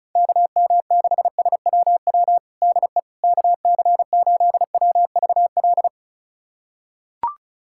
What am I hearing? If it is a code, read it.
Morse code, heard as a keyed tone, about 34 wpm, 700 Hz. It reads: KM6SWW DE KC8WVL
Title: USA Callsign DE Callsign 35wpm